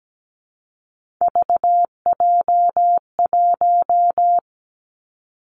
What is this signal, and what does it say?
Morse code, heard as a keyed tone, about 17 wpm, 700 Hz. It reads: VJ1
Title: Three Challenging Letter-Number Combinations 17wpm